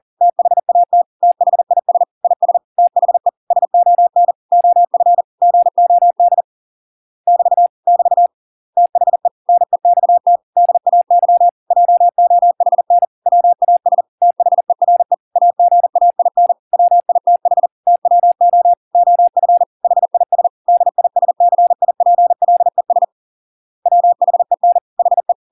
Morse code, heard as a keyed tone, about 40 wpm, 700 Hz. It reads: THAT THIS IS THE SON OF GOD <BT> <BT> THE NEXT DAY JOHN WAS THERE AGAIN WITH TWO OF HIS DISCIPLES WHEN HE